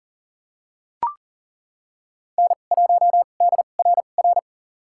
Morse code, heard as a keyed tone, about 38 wpm, spaced as for 28 wpm, 700 Hz. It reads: N1DRR
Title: USA Call Signs - 5 Characters - 2x Character Spacing - 40wpm